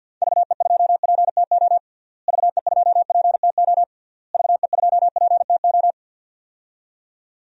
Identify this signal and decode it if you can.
Morse code, heard as a keyed tone, about 52 wpm, 700 Hz. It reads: VE2PTO VE2PTO VE2PTO
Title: DXCC-100 Call Signs - 6 Characters - Repeated 3x - 50wpm